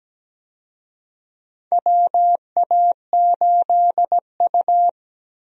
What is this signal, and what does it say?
Morse code, heard as a keyed tone, about 17 wpm, 700 Hz. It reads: WA8U
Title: USA Call Signs - 4 Characters - Repeated 2x - 17wpm